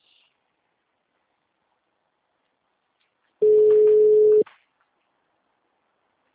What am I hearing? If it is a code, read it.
Ringback tone (Europe, 425 Hz).
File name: out- -2003- - - .261